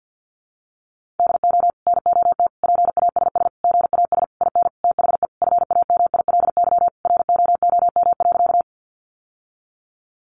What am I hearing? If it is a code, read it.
Morse code, heard as a keyed tone, about 50 wpm, 700 Hz. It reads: DO NOT PASS GAS IN THE FAMILY ROOM.
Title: Sentences from Top 1100 Words 50wpm